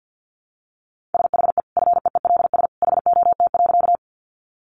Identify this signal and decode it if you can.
Morse code, heard as a keyed tone, about 50 wpm, 700 Hz. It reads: SHE FEELS HOT.
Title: Additional Easy Phrases 50wpm